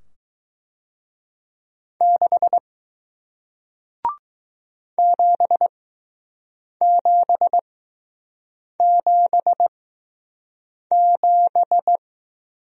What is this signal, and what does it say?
Morse code, heard as a keyed tone, about 20 wpm, spaced as for 6 wpm, 700 Hz. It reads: 6 7777